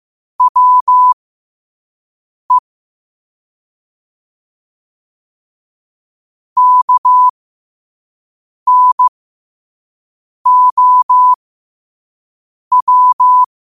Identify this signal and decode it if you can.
Morse code, heard as a keyed tone, about 15 wpm, spaced as for 5 wpm, 1000 Hz. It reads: WE KNOW
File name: qotdfiles / 5wpm